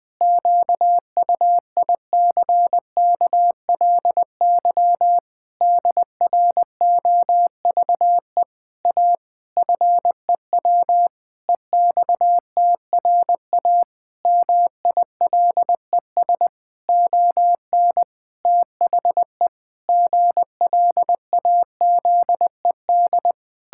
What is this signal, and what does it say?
Morse code, heard as a keyed tone, about 21 wpm, 700 Hz. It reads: QUICKLY DROVE A FEW EXTRA MILES ON THE GLAZED
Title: Pangram Challenge v2 - 20wpm